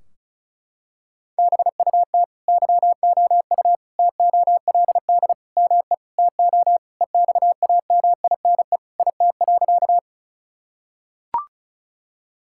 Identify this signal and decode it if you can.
Morse code, heard as a keyed tone, about 34 wpm, 700 Hz. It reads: BUT YOU TOLD ME TO EXAMINE IT.
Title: Sentences from Top 1600 Words 35wpm